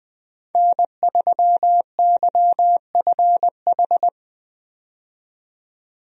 Morse code, heard as a keyed tone, about 20 wpm, 700 Hz. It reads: N3YFH
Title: USA Call Signs - 5 Characters - 20wpm